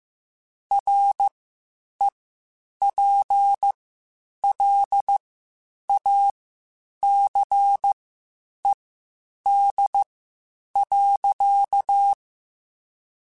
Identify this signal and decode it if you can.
Morse code, heard as a keyed tone, about 15 wpm, spaced as for 8 wpm, 800 Hz. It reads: REPLACED.
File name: Spaced / Chuckles Spaced-mp3s / Squeaky sp @13